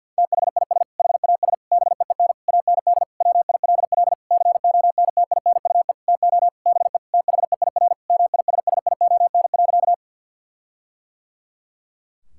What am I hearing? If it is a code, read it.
Morse code, heard as a keyed tone, about 50 wpm, 700 Hz. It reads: THIS HAS BEEN AND WILL CONTINUE TO BE THEIR MISSION.
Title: Sentences from Top 1600 Words 50wpm